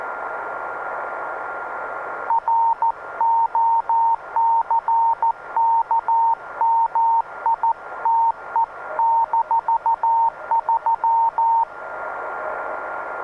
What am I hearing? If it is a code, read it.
Morse code, heard as a keyed tone, about 14 wpm, 920 Hz. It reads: ROCKMITE-3